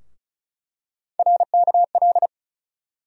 Morse code, heard as a keyed tone, about 34 wpm, 700 Hz. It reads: RKL
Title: USA Call Sign Postfixes - 3 Letters 35wpm